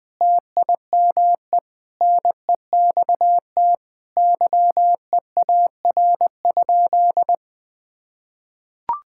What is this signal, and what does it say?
Morse code, heard as a keyed tone, about 19 wpm, 700 Hz. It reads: TIME NEXT YEAR?
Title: Sentences from Top 1000 Words - Review Part 5 - 20wpm